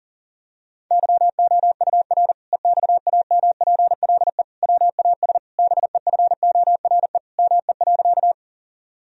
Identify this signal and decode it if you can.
Morse code, heard as a keyed tone, about 40 wpm, 700 Hz. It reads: YOUR EXAMPLE WAS BEFORE ME.
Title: Sentences from Top 1000 Words - Review Part 1 - 40wpm